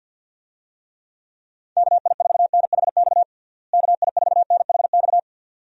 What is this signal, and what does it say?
Morse code, heard as a keyed tone, about 50 wpm, 700 Hz. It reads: KI4NHX KI4NHX
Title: USA Call Signs - 6 Characters - Repeated 2x - 50wpm